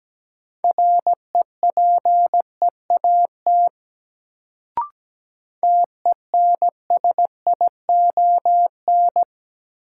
Morse code, heard as a keyed tone, about 17 wpm, 700 Hz. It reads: REPEAT TENSION
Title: Sentences from Top 1600 Words - New Word Drill - 17wpm